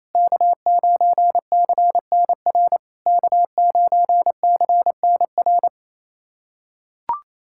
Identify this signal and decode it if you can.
Morse code, heard as a keyed tone, about 28 wpm, 690 Hz. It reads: K9CNR K9CNR
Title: SOTA QSO Practice 28wpm